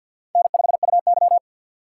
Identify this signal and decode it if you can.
Morse code, heard as a keyed tone, about 51 wpm, 700 Hz. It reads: N5UY
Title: USA Call Signs - 4x Character Spacing - 50wpm